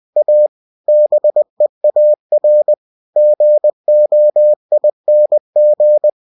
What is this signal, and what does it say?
Morse code, heard as a keyed tone, about 20 wpm, 600 Hz. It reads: A BEAR GOING